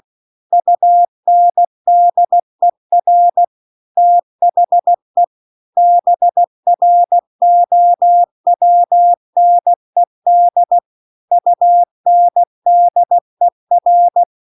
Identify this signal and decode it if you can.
Morse code, heard as a keyed tone, about 16 wpm, 700 Hz. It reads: UNDER THE BROWNED UNDER